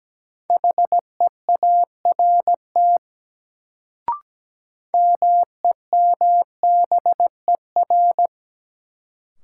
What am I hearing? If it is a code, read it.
Morse code, heard as a keyed tone, about 17 wpm, 700 Hz. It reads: HEART MEMBER
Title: Sets of 3 Words - New Word Drill - 17wpm